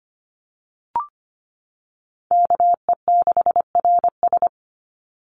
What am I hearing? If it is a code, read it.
Morse code, heard as a keyed tone, about 25 wpm, 700 Hz. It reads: KE6RS
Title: NA SOTA Chaser Callsigns 25wpm